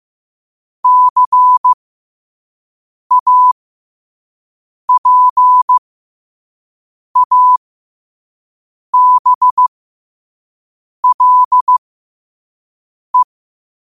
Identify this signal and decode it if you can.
Morse code, heard as a keyed tone, about 15 wpm, spaced as for 5 wpm, 1000 Hz. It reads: CAPABLE
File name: qotdfiles / 5wpm